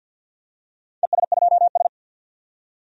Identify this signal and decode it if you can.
Morse code, heard as a keyed tone, about 47 wpm, 700 Hz. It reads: ES2S
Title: DXCC-100 Call Signs - 4 Characters - 50wpm